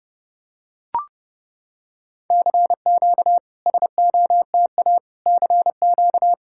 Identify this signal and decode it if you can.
Morse code, heard as a keyed tone, about 30 wpm, 690 Hz. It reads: CQ SOTA CQ
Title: SOTA QSO Practice 30wpm